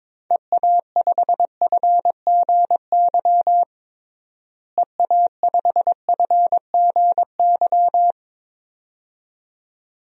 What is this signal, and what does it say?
Morse code, heard as a keyed tone, about 22 wpm, 700 Hz. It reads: EA5FGY EA5FGY
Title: DXCC-100 Call Signs - 6 Characters - Repeated 2x - 22wpm